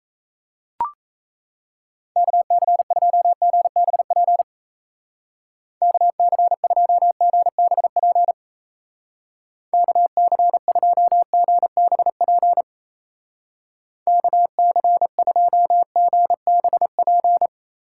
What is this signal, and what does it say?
Morse code, heard as a keyed tone, about 36 wpm, 700 Hz. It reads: KC2GBP KC2GBP KC2GBP KC2GBP